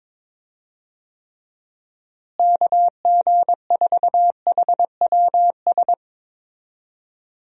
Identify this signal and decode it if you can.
Morse code, heard as a keyed tone, about 23 wpm, 700 Hz. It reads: KG4HWS
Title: USA Call Signs - 6 Characters - Repeated 2x - 22wpm